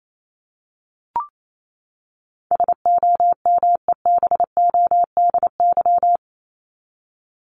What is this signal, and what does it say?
Morse code, heard as a keyed tone, about 28 wpm, 700 Hz. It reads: SOMEBODY
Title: Top 1751-2000 Words 28wpm